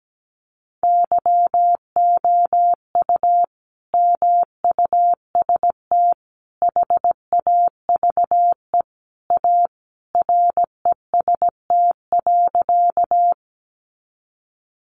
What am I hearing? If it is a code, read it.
Morse code, heard as a keyed tone, about 17 wpm, 700 Hz. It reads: YOU MUST HAVE A REST.
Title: Sentences from Top 500 Words 17wpm